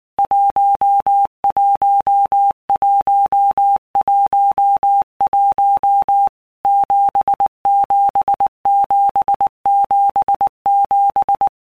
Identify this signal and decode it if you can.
Morse code, heard as a keyed tone, about 19 wpm, 800 Hz. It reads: 11111 77777
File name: All Groups 1-minute Blocks / Groups@18 / JZ17